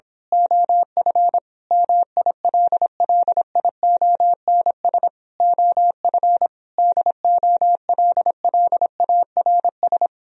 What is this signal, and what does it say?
Morse code, heard as a keyed tone, about 26 wpm, 700 Hz. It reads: OF MILLIONS OF DOLLARS
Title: Speed-Racing - Sets of 5 Words 22wpm